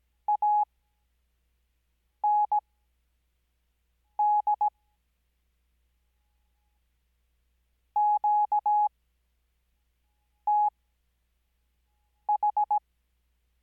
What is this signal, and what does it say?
Morse code, heard as a keyed tone, about 17 wpm, spaced as for 5 wpm, 840 Hz. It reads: AND QTH
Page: Farnsworth Spacing: 15WPM Character speed Spaced at 5WPM.